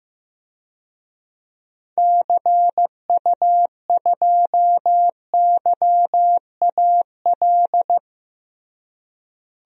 Morse code, heard as a keyed tone, about 15 wpm, 700 Hz. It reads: CU2YAL
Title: DXCC-100 Call Signs - 6 Characters - Repeated 2x - 15wpm